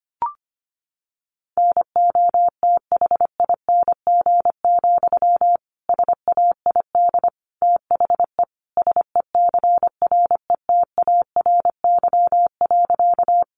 Morse code, heard as a keyed tone, about 25 wpm, 700 Hz. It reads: NOTHING, SAID THE SECRETARY.
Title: Sentences from Top 1300 Words 25wpm